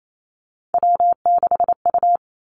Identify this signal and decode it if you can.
Morse code, heard as a keyed tone, about 28 wpm, 700 Hz. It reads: W6U
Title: Three Challenging Letter-Number Combinations 28wpm